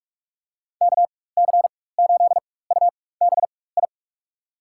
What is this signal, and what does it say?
Morse code, heard as a keyed tone, about 46 wpm, spaced as for 21 wpm, 700 Hz. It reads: KC8UBI
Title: USA Call Signs - 6 Characters - 4x Character Spacing - 45wpm